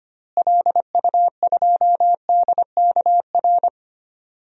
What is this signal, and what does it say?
Morse code, heard as a keyed tone, about 25 wpm, 700 Hz. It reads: LU2DKR
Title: DXCC-100 Call Signs - 6 Characters - Repeated 2x - 25wpm